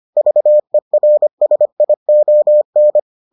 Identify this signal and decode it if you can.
Morse code, heard as a keyed tone, about 25 wpm, 600 Hz. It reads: VERSION